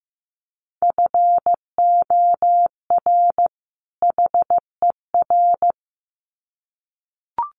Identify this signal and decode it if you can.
Morse code, heard as a keyed tone, about 15 wpm, 700 Hz. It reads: FOR HER
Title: Sets of 2 Words 15wpm